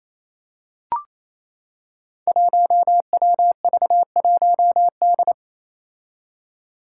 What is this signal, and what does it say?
Morse code, heard as a keyed tone, about 28 wpm, 700 Hz. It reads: 1WV1D
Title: Five Challenging Letter-Number Combinations 28wpm